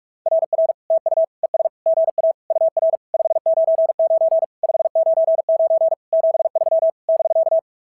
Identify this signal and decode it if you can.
Morse code, heard as a keyed tone, about 46 wpm, 690 Hz. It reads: RR TU ES GA UR 599 599 73 <BK>
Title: SOTA QSO Practice 45wpm